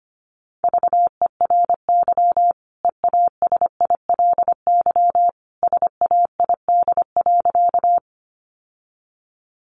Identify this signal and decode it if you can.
Morse code, heard as a keyed tone, about 25 wpm, 700 Hz. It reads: VERY EASILY SAID.
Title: Sentences from Top 900 Words 25wpm